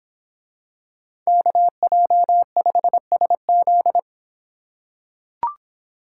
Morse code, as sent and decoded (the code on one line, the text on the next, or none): -.- .--- ..... ... --..
KJ5SZ